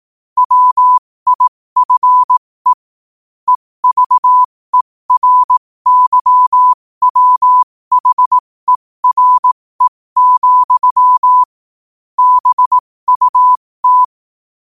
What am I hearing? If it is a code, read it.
Morse code, heard as a keyed tone, about 18 wpm, spaced as for 15 wpm, 1000 Hz. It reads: WIFE EVERYWHERE, BUT
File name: qotdfiles / 15wpm